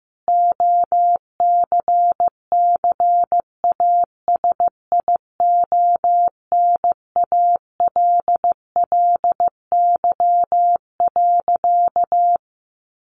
Morse code, heard as a keyed tone, about 15 wpm, 700 Hz. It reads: OCCASIONALLY.
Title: Sentences from Top 1300 Words 15wpm